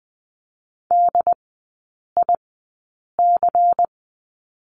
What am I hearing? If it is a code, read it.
Morse code, heard as a keyed tone, about 20 wpm, spaced as for 8 wpm, 700 Hz. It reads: DIC
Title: 107 Lesson 18e - Letter m - Three Characters 20wpm